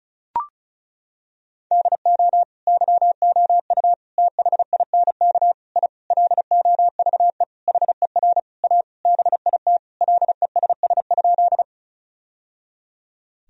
Morse code, heard as a keyed tone, about 34 wpm, 700 Hz. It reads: DO YOU THINK I LOVE HER A BIT LESS?
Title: Sentences from Top 1000 Words - Review Part 2 - 35wpm